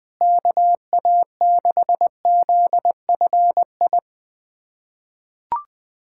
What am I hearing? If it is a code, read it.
Morse code, heard as a keyed tone, about 20 wpm, 700 Hz. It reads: KA6ZFI
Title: USA Callsign DE Callsign - 6 Characters - 20wpm